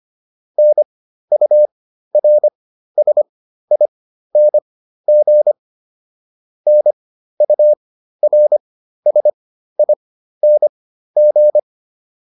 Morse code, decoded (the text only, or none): NURSING NURSING